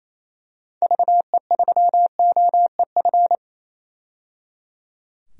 Morse code, heard as a keyed tone, about 28 wpm, 700 Hz. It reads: VE3OEF
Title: DXCC-100 Call Signs - 6 Characters - 28wpm